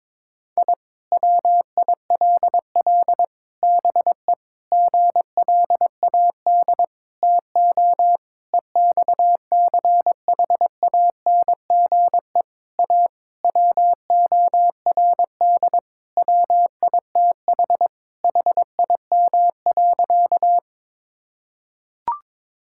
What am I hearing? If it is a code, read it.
Morse code, heard as a keyed tone, about 22 wpm, 700 Hz. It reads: I WILL BE GLAD TO EXCHANGE A WORD WITH HIM.
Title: Sentences from Top 1000 Words - Review Part 6 - 22wpm